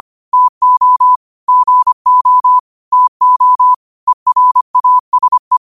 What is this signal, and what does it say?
Morse code, heard as a keyed tone, about 25 wpm, 1000 Hz. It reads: TO GO TO ERASE
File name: qotdfiles / 25wpm